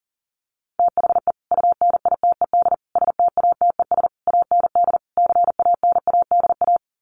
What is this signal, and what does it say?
Morse code, heard as a keyed tone, about 37 wpm, 700 Hz. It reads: THE UNITED STATES AND CANADA
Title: Sets of 5 Words 40wpm